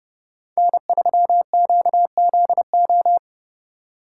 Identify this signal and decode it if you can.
Morse code, heard as a keyed tone, about 32 wpm, 700 Hz. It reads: N3QZO